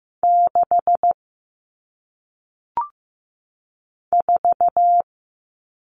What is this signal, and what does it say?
Morse code, heard as a keyed tone, about 15 wpm, 700 Hz. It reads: 6 4
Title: Numbers - 1-Digit 15wpm